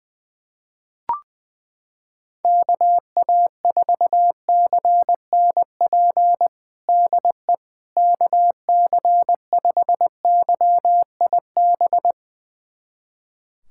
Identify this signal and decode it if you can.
Morse code, heard as a keyed tone, about 20 wpm, 700 Hz. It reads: KA4CNP DE KC5YIB
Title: USA Callsign DE Callsign - 6 Characters - 20wpm